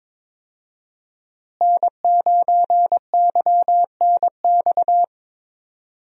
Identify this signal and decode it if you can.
Morse code, heard as a keyed tone, about 22 wpm, 700 Hz. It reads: N9YNX
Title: USA Call Signs - 5 Characters - Repeated 3x - 22wpm